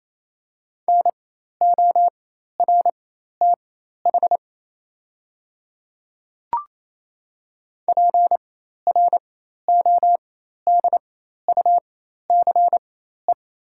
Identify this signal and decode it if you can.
Morse code, heard as a keyed tone, about 27 wpm, spaced as for 13 wpm, 690 Hz. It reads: NORTH PRODUCE
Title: 4x Character Spacing - Top 500 Words - 28wpm